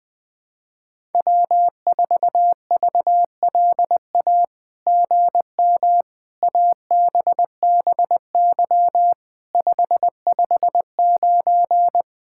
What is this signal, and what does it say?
Morse code, heard as a keyed tone, about 20 wpm, 700 Hz. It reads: W4VLA GM ABBY 559
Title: SOTA QSO Practice 20wpm